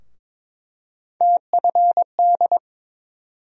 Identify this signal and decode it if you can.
Morse code, heard as a keyed tone, about 22 wpm, 700 Hz. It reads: TFD
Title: USA Call Sign Postfixes - 3 Letters 22wpm